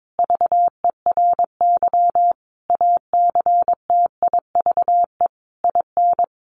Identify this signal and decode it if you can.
Morse code, heard as a keyed tone, about 22 wpm, 700 Hz. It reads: VERY ACTIVE IN
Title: Sentences from Top 1500 Words 22wpm